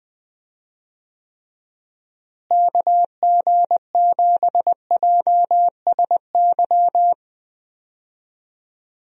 Morse code, heard as a keyed tone, about 21 wpm, 700 Hz. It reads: KG7JSY
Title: USA Call Signs - 6 Characters - Repeated 2x - 20wpm